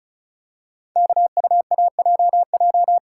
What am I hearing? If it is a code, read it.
Morse code, heard as a keyed tone, about 36 wpm, 700 Hz. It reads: KUAJJ
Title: Code Groups - 5 Letters - 35wpm